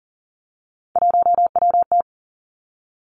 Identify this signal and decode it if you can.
Morse code, heard as a keyed tone, about 38 wpm, 690 Hz. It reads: 1WT
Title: Three Challenging Letter-Number Combinations 40wpm